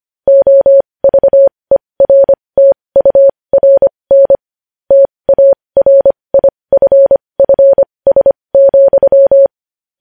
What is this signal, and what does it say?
Morse code, heard as a keyed tone, about 25 wpm, spaced as for 20 wpm, 550 Hz. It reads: OVERTURN TARIFFS,
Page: news-headlines-in-morse-code-at-20-wpm